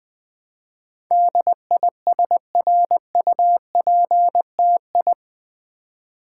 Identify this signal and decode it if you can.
Morse code, heard as a keyed tone, about 20 wpm, 700 Hz. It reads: DISRUPTI
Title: Build A Word - 4 Letters At A Time - 10-Letter Words - 20wpm